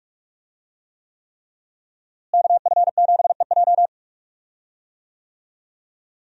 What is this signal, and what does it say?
Morse code, heard as a keyed tone, about 41 wpm, 700 Hz. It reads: KF7EJ
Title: USA Call Signs - 5 Characters - Repeated 3x - 45wpm